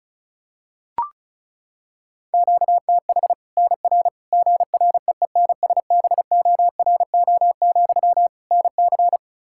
Morse code, heard as a keyed tone, about 36 wpm, 700 Hz. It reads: QTH NR GREENSBORO, NC
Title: Practice QSO - Exchange 1 - Beginning - 35wpm